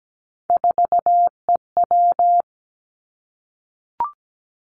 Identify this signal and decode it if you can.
Morse code, heard as a keyed tone, about 17 wpm, 700 Hz. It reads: 4EW
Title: Three Challenging Letter-Number Combinations 17wpm